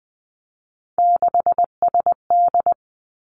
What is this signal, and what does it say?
Morse code, heard as a keyed tone, about 20 wpm, 700 Hz. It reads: 6SD
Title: Three Challenging Letter-Number Combinations 20wpm